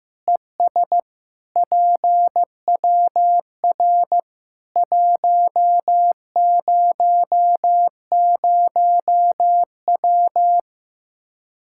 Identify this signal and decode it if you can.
Morse code, heard as a keyed tone, about 15 wpm, 700 Hz. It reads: ES PWR 100W
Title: Practice QSO - Exchange 2 - Info - 15wpm